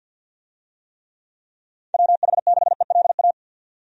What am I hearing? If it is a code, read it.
Morse code, heard as a keyed tone, about 48 wpm, 700 Hz. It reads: WH6ELA
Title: USA Call Signs - 6 Characters - 50wpm